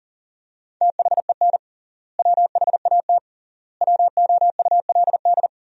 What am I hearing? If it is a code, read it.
Morse code, heard as a keyed tone, about 40 wpm, 700 Hz. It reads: THEN WHAT WOULD
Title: 3x Word-Spacing - Sentences from Top 400 Words - 40wpm